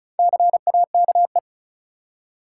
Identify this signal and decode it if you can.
Morse code, heard as a keyed tone, about 36 wpm, 700 Hz. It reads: CAKE
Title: Four Letter Words - 35wpm